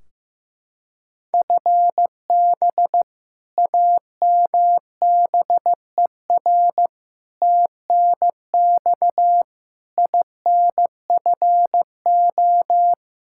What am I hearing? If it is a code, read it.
Morse code, heard as a keyed tone, about 15 wpm, 700 Hz. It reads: FB AMBER TNX INFO
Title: Practice QSO - Exchange 3 - More Info - 15wpm